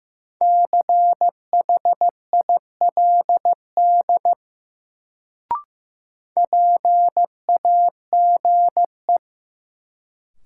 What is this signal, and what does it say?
Morse code, heard as a keyed tone, about 15 wpm, 700 Hz. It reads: CHILD PAGE
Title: Sets of 5 Words Encore - New Word Drill - 15wpm